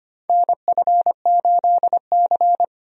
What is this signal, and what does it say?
Morse code, heard as a keyed tone, about 25 wpm, 700 Hz. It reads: NF8C
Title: USA Call Signs - 4 Characters - Repeated 3x - 25wpm